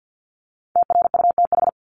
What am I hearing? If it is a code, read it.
Morse code, heard as a keyed tone, about 48 wpm, 700 Hz. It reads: TRUTH
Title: Top 751-1000 Words 50wpm